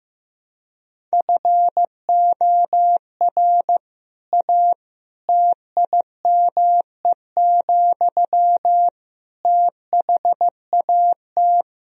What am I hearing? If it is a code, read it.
Morse code, heard as a keyed tone, about 15 wpm, 700 Hz. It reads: FOR A TIME, THAT
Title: Sentences from Top 1000 Words - Review Part 4 - 15wpm